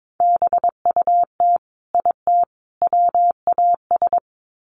Morse code, heard as a keyed tone, about 22 wpm, 700 Hz. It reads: BUT IT WAS
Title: Sets of 3 Words 22wpm